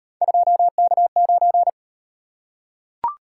Morse code, heard as a keyed tone, about 36 wpm, 700 Hz. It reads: 2K9